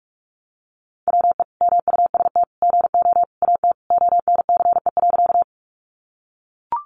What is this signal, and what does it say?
Morse code, heard as a keyed tone, about 44 wpm, 700 Hz. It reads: WE MUST GO AT ONCE.
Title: Sentences from Top 300 Words 45wpm